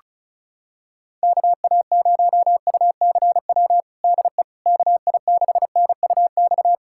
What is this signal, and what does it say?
Morse code, heard as a keyed tone, about 36 wpm, 700 Hz. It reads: KA0UCW DE KI6NUX
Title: USA Callsign DE Callsign - 6 Characters - 35wpm